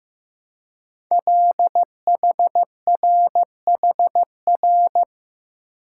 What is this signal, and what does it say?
Morse code, heard as a keyed tone, about 15 wpm, 700 Hz. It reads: LHRHR
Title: Code Groups - 5 Letter:Numbers - 15wpm